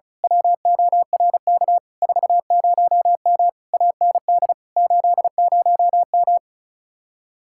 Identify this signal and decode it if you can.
Morse code, heard as a keyed tone, about 33 wpm, 700 Hz. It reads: WORK 40M AND 80M
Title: Speed-Racing Common QSO Phrases - 35wpm